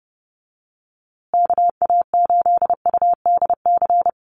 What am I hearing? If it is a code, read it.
Morse code, heard as a keyed tone, about 30 wpm, 700 Hz. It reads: KA8UDC
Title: USA Call Signs Repeated 2x 30wpm